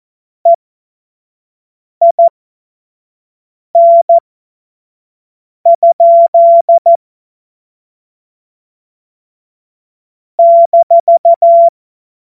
Morse code, heard as a keyed tone, about 14 wpm, spaced as for 5 wpm, 680 Hz. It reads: EIN? -
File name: Morsekurs / Witze / Witz 25. Juli 2023 25 F70